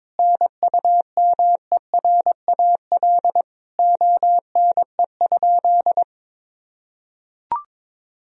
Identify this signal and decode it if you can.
Morse code, heard as a keyed tone, about 22 wpm, 700 Hz. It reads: NUMERAL ONE?
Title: Sentences from Top 1000 Words - Review Part 2 - 22wpm